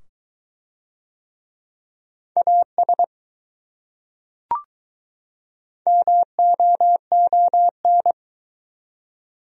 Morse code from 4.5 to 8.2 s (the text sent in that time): MOON